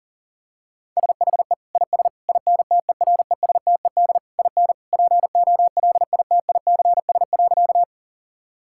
Morse code, heard as a keyed tone, about 40 wpm, 700 Hz. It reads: SHE IS INTERESTED IN POLITICS.
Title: Sentences from Top 1000 Words - Review Part 6 - 40wpm